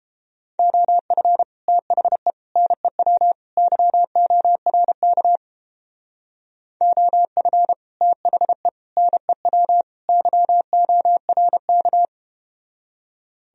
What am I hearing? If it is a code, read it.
Morse code, heard as a keyed tone, about 32 wpm, 700 Hz. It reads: OF THE NEW YORK OF THE NEW YORK